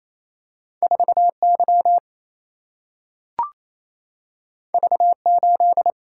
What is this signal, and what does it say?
Morse code, heard as a keyed tone, about 28 wpm, 700 Hz. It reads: 4 Y V 8
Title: Single-Letter Single-Number 28wpm